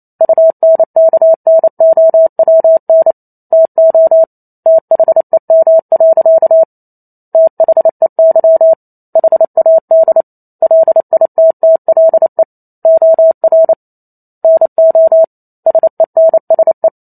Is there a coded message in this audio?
Yes: Morse code from 0.2 to 16.9 s, heard as a keyed tone, about 28 wpm, 650 Hz. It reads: UNKNOWN TO THEM. THEY HAD LITTLE OR NO SENSE